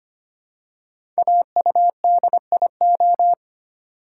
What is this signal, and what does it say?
Morse code, heard as a keyed tone, about 25 wpm, 700 Hz. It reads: AUDIO
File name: 5 Letter Words - First 3 Letters Given - 25wpm